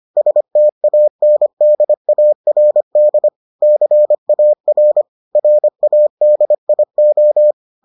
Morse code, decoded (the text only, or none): STANDARD CAR RADIO